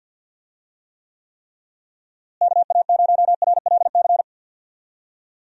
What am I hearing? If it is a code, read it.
Morse code, heard as a keyed tone, about 46 wpm, 700 Hz. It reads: KA0RLC
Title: USA Call Signs - 6 Characters - Repeated 3x - 50wpm